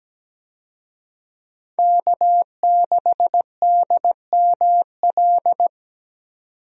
Morse code, heard as a keyed tone, about 17 wpm, 700 Hz. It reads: K6DML
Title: USA Call Signs - 5 Characters - 17wpm